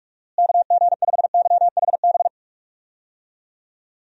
Morse code, heard as a keyed tone, about 42 wpm, 700 Hz. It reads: KG5YHB
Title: USA Call Signs - 6 Characters - Repeated 2x - 45wpm